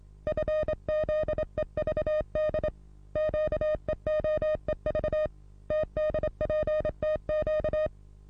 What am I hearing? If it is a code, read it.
Morse code, heard as a keyed tone, about 22 wpm, 620 Hz. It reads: FZEVD QEOEV TDPTQ
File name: repos / Morsekurs / DJ4UF Morsen27